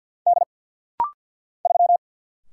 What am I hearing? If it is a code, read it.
Morse code, heard as a keyed tone, about 50 wpm, 700 Hz. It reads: D 3
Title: Single Letter-Number Alt - Rapid-Fire 50wpm